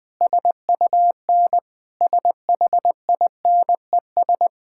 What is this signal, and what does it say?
Morse code, heard as a keyed tone, about 20 wpm, 700 Hz. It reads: SUN SHINES
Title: Sentences with 4-Word Phrases - 20wpm